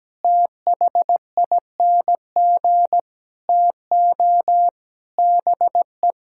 Morse code, heard as a keyed tone, about 17 wpm, 700 Hz. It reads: THING TO BE
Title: Sentences from Top 1000 Words - Review Part 5 - 17wpm